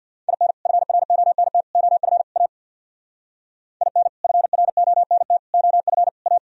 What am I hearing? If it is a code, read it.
Morse code, heard as a keyed tone, about 64 wpm, 700 Hz. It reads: IN FRONT OF A IN FRONT OF A